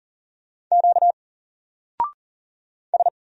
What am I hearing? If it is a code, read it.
Morse code, heard as a keyed tone, about 39 wpm, 700 Hz. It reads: Q S
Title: Single Letter-Number 40wpm